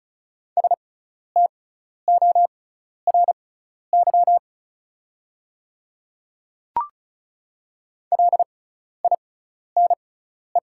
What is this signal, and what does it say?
Morse code, heard as a keyed tone, about 34 wpm, spaced as for 12 wpm, 700 Hz. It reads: STORY LINE
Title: 6x Character Spacing - Top 300 Words - 35wpm